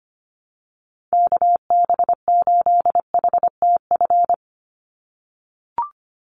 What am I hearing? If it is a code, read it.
Morse code, heard as a keyed tone, about 25 wpm, 700 Hz. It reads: KB8HTF
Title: USA Call Signs 25wpm